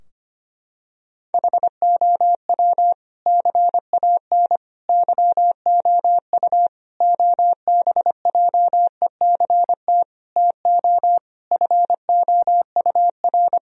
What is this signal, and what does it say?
Morse code, heard as a keyed tone, about 25 wpm, 700 Hz. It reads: HOW CAN YOU OBJECT TO FOUR
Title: Sentences from Top 1000 Words - Review Part 3 - 25wpm